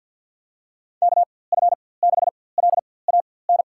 Morse code, heard as a keyed tone, about 51 wpm, spaced as for 23 wpm, 700 Hz. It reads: KF6LAN
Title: USA Call Signs - 6 Characters - 4x Character Spacing - 50wpm